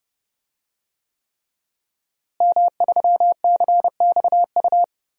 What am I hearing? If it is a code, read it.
Morse code, heard as a keyed tone, about 30 wpm, 700 Hz. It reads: M3CXU
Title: DXCC-100 Call Signs - 5 Characters - Repeated 3x - 30wpm